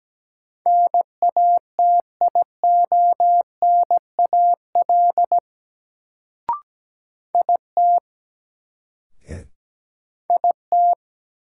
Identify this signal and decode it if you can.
Morse code, heard as a keyed tone, about 17 wpm, 700 Hz. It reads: NATIONAL IT IT
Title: Sets of 2 Words - New Word Drill - 17wpm